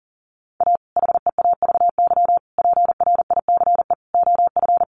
Morse code, heard as a keyed tone, about 40 wpm, 700 Hz. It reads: A HEAVY PRICE OF
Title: Sentences from Top 1100 Words 40wpm